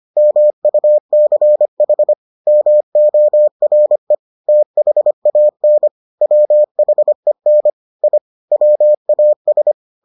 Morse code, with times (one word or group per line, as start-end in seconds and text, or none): 0.2-2.1 MUCH
2.5-4.2 MORE
4.5-5.9 THAN
6.2-7.7 WHEN
8.0-8.2 I
8.5-9.7 WAS